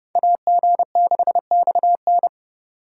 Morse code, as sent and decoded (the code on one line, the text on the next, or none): .- --. -.... -..- -.
AG6XN